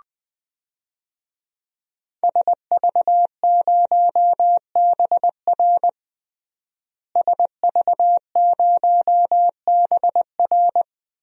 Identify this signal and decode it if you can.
Morse code, heard as a keyed tone, about 19 wpm, 700 Hz. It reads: SV0BR SV0BR
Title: DXCC-100 Call Signs - 5 Characters - Repeated 3x - 20wpm